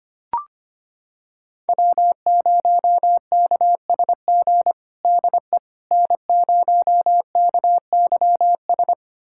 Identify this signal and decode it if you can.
Morse code, heard as a keyed tone, about 25 wpm, 700 Hz. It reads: W0KSG DE N0KYS
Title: USA Callsign DE Callsign - 5 Characters - 25wpm